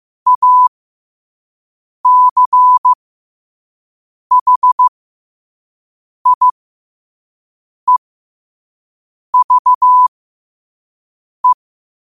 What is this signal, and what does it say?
Morse code, heard as a keyed tone, about 15 wpm, spaced as for 5 wpm, 1000 Hz. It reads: ACHIEVE
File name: qotdfiles / 5wpm